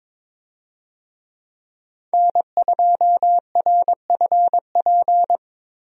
Morse code, heard as a keyed tone, about 21 wpm, 700 Hz. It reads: N2RFP
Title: USA Call Signs - 5 Characters - Repeated 3x - 22wpm